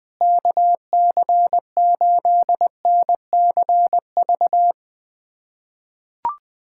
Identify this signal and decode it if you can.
Morse code, heard as a keyed tone, about 21 wpm, 700 Hz. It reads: KC8NCV
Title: USA Callsign DE Callsign - 6 Characters - 20wpm